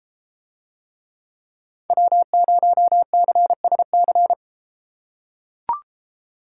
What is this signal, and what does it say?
Morse code, heard as a keyed tone, about 31 wpm, 690 Hz. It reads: W0CSC